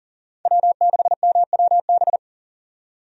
Morse code, heard as a keyed tone, about 40 wpm, 700 Hz. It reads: W6MWB
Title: USA Call Signs - 5 Characters - 40wpm